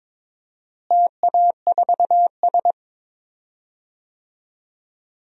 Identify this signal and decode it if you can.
Morse code, heard as a keyed tone, about 22 wpm, 700 Hz. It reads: TA4S
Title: DXCC-100 Call Signs - 4 Characters - Repeated 3x - 22wpm